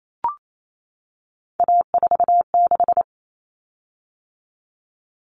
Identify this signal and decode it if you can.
Morse code, heard as a keyed tone, about 28 wpm, 700 Hz. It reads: A46
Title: Common Cut Numbers - 3-Digit 28wpm